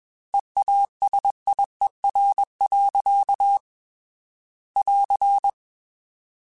Morse code, heard as a keyed tone, about 21 wpm, 800 Hz. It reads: EASIER. <AR>
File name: Technical Subjects / Technical Subjects-mp3s / AND Gate @20